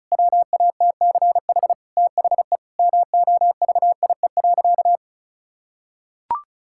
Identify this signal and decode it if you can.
Morse code, heard as a keyed tone, about 34 wpm, 700 Hz. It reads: WATCH THE MOVIE.
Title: Sentences from Top 1000 Words - Review Part 5 - 35wpm